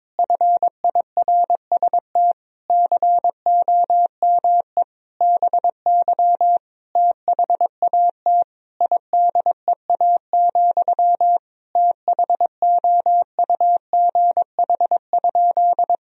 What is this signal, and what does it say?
Morse code, heard as a keyed tone, about 21 wpm, 700 Hz. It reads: FIRST COME BY THAT IDEA, THOUGH?
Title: Sentences from Top 1000 Words - Review Part 2 - 22wpm